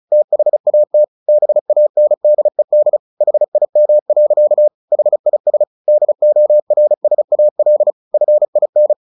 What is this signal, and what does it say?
Morse code, heard as a keyed tone, about 34 wpm, 600 Hz. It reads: THAT BANDED HIM. HIS DORSAL FIN
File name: the old man and the sea 2 35wpm